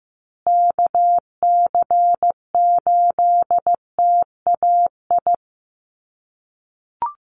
Morse code, heard as a keyed tone, about 15 wpm, 700 Hz. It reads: KC8TAI
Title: USA Call Signs Repeated 3x 15wpm